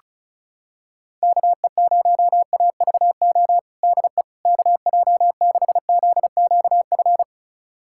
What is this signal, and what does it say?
Morse code, heard as a keyed tone, about 34 wpm, 700 Hz. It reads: KE0AVO DE KJ6ZQF
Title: USA Callsign DE Callsign 35wpm